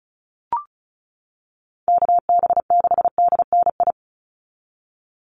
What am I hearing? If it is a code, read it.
Morse code, heard as a keyed tone, about 34 wpm, 700 Hz. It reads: KB6DNI
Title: USA Call Signs 35wpm